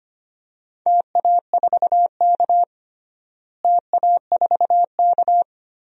Morse code, heard as a keyed tone, about 25 wpm, 700 Hz. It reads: TA4K TA4K
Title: DXCC-100 Call Signs - 4 Characters - Repeated 2x - 25wpm